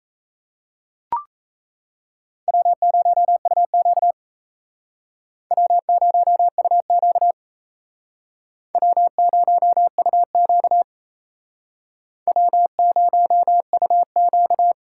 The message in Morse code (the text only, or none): W0UQ W0UQ W0UQ W0UQ